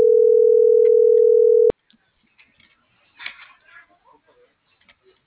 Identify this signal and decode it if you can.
Ringback tone (North America), 440 + 480 Hz.
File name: CENTRAL AGUILERA